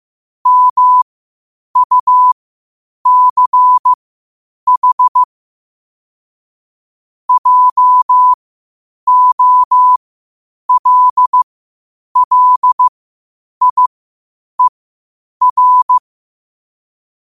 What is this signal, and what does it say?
Morse code, heard as a keyed tone, about 15 wpm, spaced as for 8 wpm, 1000 Hz. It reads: MUCH JOLLIER
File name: qotdfiles / 8wpm